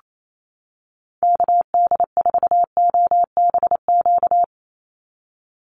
Morse code, heard as a keyed tone, about 28 wpm, 700 Hz. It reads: KD4OBQ
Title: USA Call Signs 28wpm